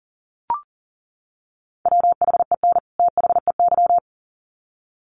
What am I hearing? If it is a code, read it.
Morse code, heard as a keyed tone, about 40 wpm, 700 Hz. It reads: WHEN THEY
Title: Sets of 2 Words 40wpm